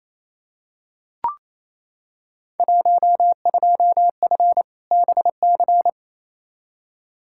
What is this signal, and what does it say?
Morse code, heard as a keyed tone, about 28 wpm, 700 Hz. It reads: 12F BC
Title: Field Day Exchanges Encore - 28wpm